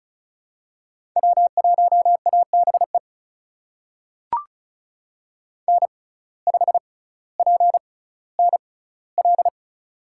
Morse code, heard as a keyed tone, about 36 wpm, spaced as for 18 wpm, 700 Hz. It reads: W1ABE N5PNL
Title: USA Call Signs - 5 Characters - 6x Character Spacing - 35wpm